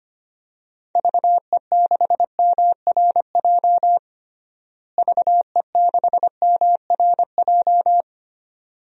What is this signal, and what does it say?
Morse code, heard as a keyed tone, about 25 wpm, 700 Hz. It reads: VE6MRJ VE6MRJ
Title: DXCC-100 Call Signs - 6 Characters - Repeated 2x - 25wpm